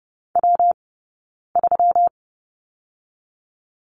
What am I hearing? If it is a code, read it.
Morse code, heard as a keyed tone, about 30 wpm, 700 Hz. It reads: W 3
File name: Lesson 20d - Number 3 - Two Characters 30wpm